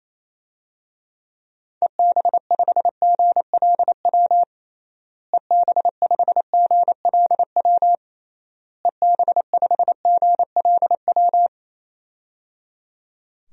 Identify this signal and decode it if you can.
Morse code, heard as a keyed tone, about 28 wpm, 700 Hz. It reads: EB5GLW EB5GLW EB5GLW
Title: DXCC-100 Call Signs - 6 Characters - Repeated 3x - 28wpm